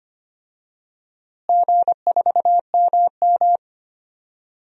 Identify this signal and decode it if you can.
Morse code, heard as a keyed tone, about 25 wpm, 700 Hz. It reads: G4MM
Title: DXCC-100 Call Signs - 4 Characters - Repeated 2x - 25wpm